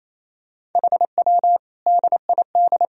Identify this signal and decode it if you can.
Morse code, heard as a keyed tone, about 28 wpm, 700 Hz. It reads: HW DID
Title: Sentences from Top 100 Words and Abbreviations 28wpm